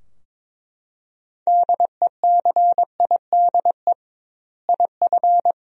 Morse code, heard as a keyed tone, about 22 wpm, 700 Hz. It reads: DECIDE IF
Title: 2x Word-Spacing - Sentences from Top 500 Words - 22wpm